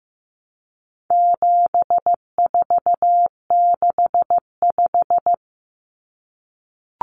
Morse code, heard as a keyed tone, about 15 wpm, 700 Hz. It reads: 7465
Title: Numbers - 4-Digit 15wpm